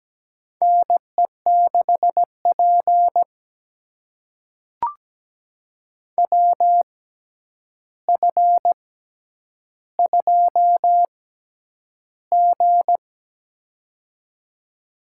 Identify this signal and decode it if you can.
Morse code, heard as a keyed tone, about 17 wpm, spaced as for 9 wpm, 700 Hz. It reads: NE6P WF2G
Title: USA Call Signs - 4 Characters - 6x Character Spacing - 17wpm